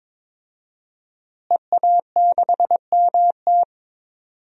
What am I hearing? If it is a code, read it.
Morse code, heard as a keyed tone, about 22 wpm, 700 Hz. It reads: EA6MT
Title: DXCC-100 Call Signs - 5 Characters - Repeated 2x - 22wpm